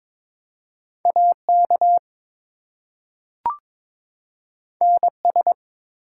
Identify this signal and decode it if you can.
Morse code, heard as a keyed tone, about 22 wpm, 700 Hz. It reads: AK NS
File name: ARRL Sections v2 - 22wpm